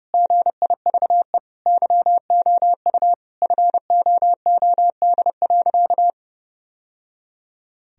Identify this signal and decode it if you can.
Morse code, heard as a keyed tone, about 30 wpm, 700 Hz. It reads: GIVE YOU FOOD.
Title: Sentences from Top 1000 Words - Review Part 1 - 30wpm